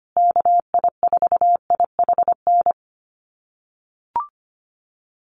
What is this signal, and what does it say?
Morse code, heard as a keyed tone, about 25 wpm, 700 Hz. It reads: KI4IHN
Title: USA Call Signs Repeated 3x 25wpm